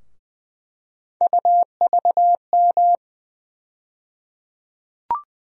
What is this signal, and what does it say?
Morse code, heard as a keyed tone, about 20 wpm, 700 Hz. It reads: UVM
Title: USA Call Sign Postfixes - 3 Letters 20wpm